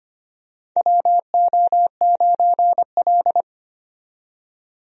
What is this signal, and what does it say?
Morse code, heard as a keyed tone, about 25 wpm, 700 Hz. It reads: WO9L
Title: USA Call Signs - 4 Characters - 25wpm